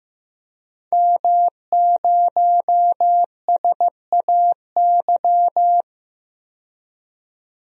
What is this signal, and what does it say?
Morse code, heard as a keyed tone, about 15 wpm, 700 Hz. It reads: M0SAY
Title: DXCC-100 Call Signs - 5 Characters - 15wpm